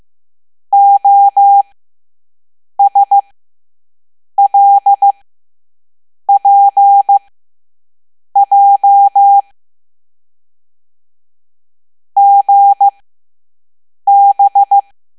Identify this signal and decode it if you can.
Morse code, heard as a keyed tone, about 15 wpm, spaced as for 6 wpm, 800 Hz. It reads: OSLPJ GB
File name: Morse / Level7-All-A